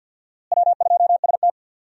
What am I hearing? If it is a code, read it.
Morse code, heard as a keyed tone, about 50 wpm, 700 Hz. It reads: W2ST
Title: USA Call Signs - 4 Characters - Repeated 2x - 50wpm